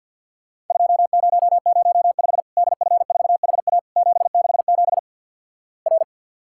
Morse code, heard as a keyed tone, about 50 wpm, 700 Hz. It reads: 2005 DF4HA 767 R
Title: Worked All Europe DX Contest - QTC Exchange - 50wpm